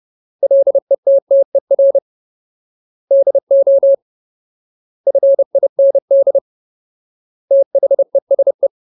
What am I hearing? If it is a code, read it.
Morse code, heard as a keyed tone, about 30 wpm, 550 Hz. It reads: LETTER DO FIND THESE
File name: Ultimate Word List / 30WPM Ultimate